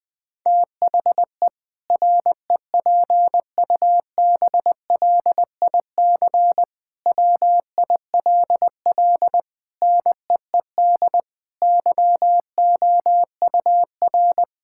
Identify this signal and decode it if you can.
Morse code, heard as a keyed tone, about 20 wpm, 700 Hz. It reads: THE REPUBLIC WILL NEED YOUR
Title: Sentences from Top 1600 Words 20wpm